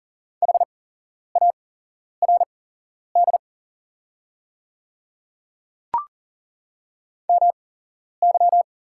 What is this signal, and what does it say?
Morse code, heard as a keyed tone, about 40 wpm, spaced as for 11 wpm, 700 Hz. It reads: HARD MY
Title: 8x Character Spacing - Top 500 Words - 40wpm